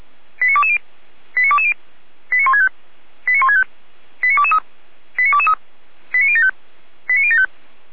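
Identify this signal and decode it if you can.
Five-tone selective call (ZVEI), heard as quick runs of five tones, tones of about 70 ms. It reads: 89220 89220 89165 89165 89223 89223 89075 89075
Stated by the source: ZVEI five-tone call 89121 received by SDR.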